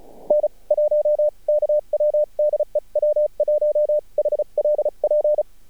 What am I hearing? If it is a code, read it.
Morse code, heard as a keyed tone, about 34 wpm, spaced as for 28 wpm, 600 Hz. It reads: N1KW DE W1HLP
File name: archived / CRSnet